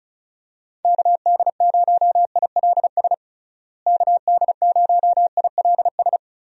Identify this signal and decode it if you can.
Morse code, heard as a keyed tone, about 36 wpm, 700 Hz. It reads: KD0ILS KD0ILS
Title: USA Call Signs - 6 Characters - Repeated 3x - 35wpm